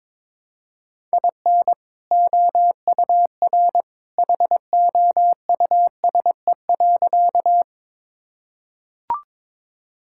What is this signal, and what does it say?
Morse code, heard as a keyed tone, about 22 wpm, 700 Hz. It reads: IN OUR HOUSE.
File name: practice / 22wpm / Sentences with 2-Word Phrases - 22wpm